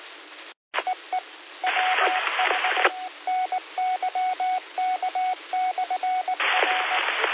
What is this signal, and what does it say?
Morse code, heard as a keyed tone, about 18 wpm, 740 Hz, with radio noise. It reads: EE WH NYK/R
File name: qsorec SimplexLogic